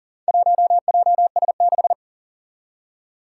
Morse code, heard as a keyed tone, about 40 wpm, 700 Hz. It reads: 1JS6
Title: Four Challenging Letter-Number Combinations 40wpm